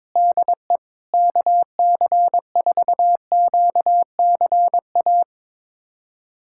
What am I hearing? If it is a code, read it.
Morse code, heard as a keyed tone, about 22 wpm, 700 Hz. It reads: DE KC4QCA
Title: USA Callsign DE Callsign 22wpm